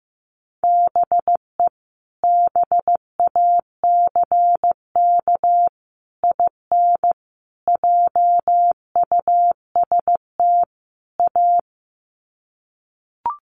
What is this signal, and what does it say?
Morse code, heard as a keyed tone, about 15 wpm, 700 Hz. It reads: BE BACK IN JUST A
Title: Sets of 5 Words Encore 15wpm